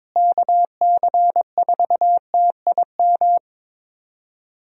Morse code, heard as a keyed tone, about 22 wpm, 700 Hz. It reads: KC4TIM
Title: USA Call Signs - 6 Characters - Repeated 3x - 22wpm